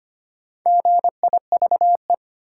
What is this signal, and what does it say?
Morse code, heard as a keyed tone, about 25 wpm, 700 Hz. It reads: GIVE
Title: Sentences from Top 200 Words - New Word Drill - 25wpm